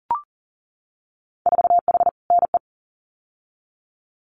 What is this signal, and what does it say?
Morse code, heard as a keyed tone, about 38 wpm, 700 Hz. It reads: 4H NE
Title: Winter Field Day Exchanges Encore 40wpm